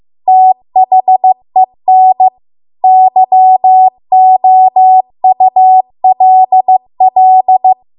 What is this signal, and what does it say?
Morse code, heard as a keyed tone, about 15 wpm, 750 Hz. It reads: THEN YOULL
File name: Archive / 15 WPM / 15WPM